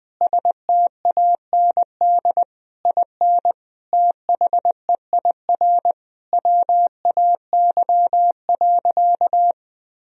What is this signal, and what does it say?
Morse code, heard as a keyed tone, about 20 wpm, 700 Hz. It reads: STAND IN THEIR WAY.
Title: Sentences from Top 1000 Words - Review Part 1 - 20wpm